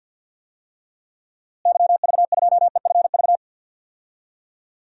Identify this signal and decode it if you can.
Morse code, heard as a keyed tone, about 50 wpm, 700 Hz. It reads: YV2EFV
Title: DXCC-100 Call Signs - 6 Characters - Repeated 2x - 50wpm